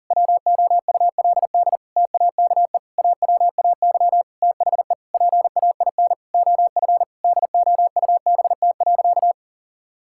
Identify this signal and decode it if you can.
Morse code, heard as a keyed tone, about 40 wpm, 700 Hz. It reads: WOULD TAKE AWAY THE PAIN OF DOUBT.
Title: Sentences from Top 1600 Words 40wpm